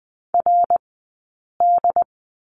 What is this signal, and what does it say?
Morse code, heard as a keyed tone, about 20 wpm, spaced as for 8 wpm, 700 Hz. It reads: RD
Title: 165 Lesson 27d - Number 7 - Two Characters 20wpm